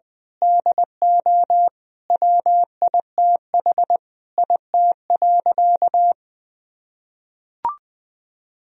Morse code, heard as a keyed tone, about 21 wpm, 700 Hz. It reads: DO WITH IT.
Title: Sentences from Top 1000 Words - Review Part 2 - 20wpm